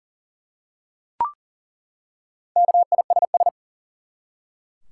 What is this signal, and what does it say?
Morse code, heard as a keyed tone, about 40 wpm, 700 Hz. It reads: KISS
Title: Four Letter Words - 40wpm